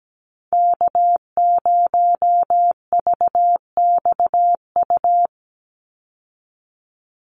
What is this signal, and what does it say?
Morse code, heard as a keyed tone, about 17 wpm, 700 Hz. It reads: K0VXU
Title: Call Signs Repeated 3x - 17wpm